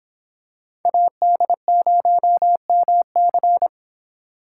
Morse code, heard as a keyed tone, about 25 wpm, 700 Hz. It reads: AD0MC